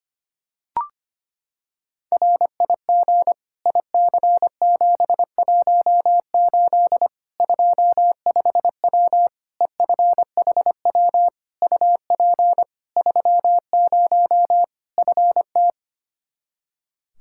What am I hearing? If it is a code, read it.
Morse code, heard as a keyed tone, about 25 wpm, 700 Hz. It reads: RIG IC718 25W EFHW UP 30 FT
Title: Practice QSO - Exchange 2 - Info - 25wpm